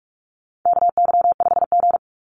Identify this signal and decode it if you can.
Morse code, heard as a keyed tone, about 44 wpm, 700 Hz. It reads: KY5G
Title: USA Call Signs 45wpm